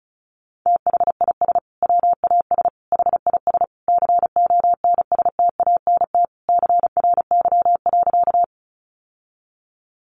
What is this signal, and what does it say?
Morse code, heard as a keyed tone, about 34 wpm, 700 Hz. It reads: THIS WAS HIS CONSTANT CRY.
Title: Sentences from Top 1500 Words 35wpm